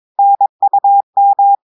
Morse code, heard as a keyed tone, about 22 wpm, 800 Hz. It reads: NUM
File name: audios / portugais / portugais texte en clair 1 vitesse 22 mots minutes